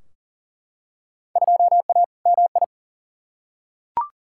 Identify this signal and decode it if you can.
Morse code, heard as a keyed tone, about 40 wpm, 700 Hz. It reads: 2A MI
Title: Field Day Exchanges Encore - 40wpm